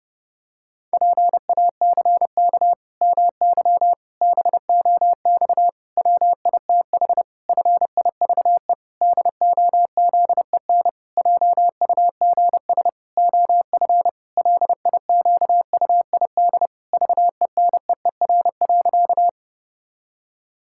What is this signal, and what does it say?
Morse code, heard as a keyed tone, about 28 wpm, 700 Hz. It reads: PACK MY BOX WITH FIVE DOZEN JUGS OF LIQUID VENEER.
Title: Pangram Challenge v2 - 30wpm